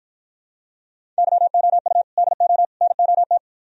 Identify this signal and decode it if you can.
Morse code, heard as a keyed tone, about 53 wpm, 700 Hz. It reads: YOU DO NOT